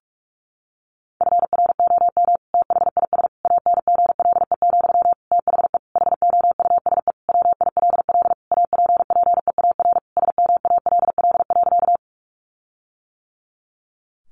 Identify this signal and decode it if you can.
Morse code, heard as a keyed tone, about 44 wpm, 700 Hz. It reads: FROM THIS ANGLE, THE HOUSE WILL APPEAR SMALL.
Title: Sentences from Top 1500 Words 45wpm